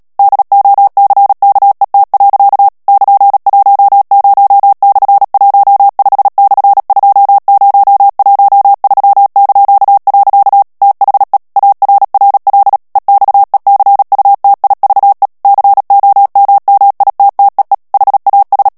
Morse code, heard as a keyed tone, about 37 wpm, 800 Hz. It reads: DOCKET. <KN>10/15/2013). THE ARRL EXECUTIVE COMMITTEE HAS
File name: morse mp3 / ANGLAIS texte clair 3 vitesse 35 mots minute